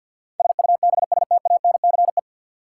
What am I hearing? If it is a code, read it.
Morse code, heard as a keyed tone, about 46 wpm, 700 Hz. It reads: SUBSTANCE
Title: Sentences from Top 1500 Words - New Word Drill - 50wpm